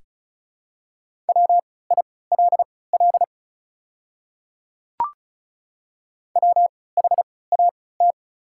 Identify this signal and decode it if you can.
Morse code, heard as a keyed tone, about 36 wpm, spaced as for 20 wpm, 690 Hz. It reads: WILL WHAT
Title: 3x Character Spacing - Top 100 Words - 35wpm